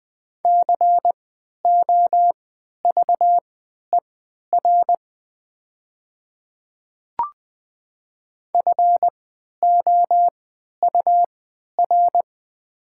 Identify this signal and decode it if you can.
Morse code, heard as a keyed tone, about 20 wpm, spaced as for 11 wpm, 690 Hz. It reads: COVER FOUR
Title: 3x Character Spacing - Top 300 Words - 20wpm